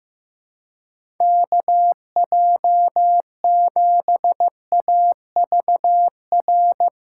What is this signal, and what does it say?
Morse code, heard as a keyed tone, about 15 wpm, 700 Hz. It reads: KJ7AVR
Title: USA Call Signs - 2x Character Spacing - 15wpm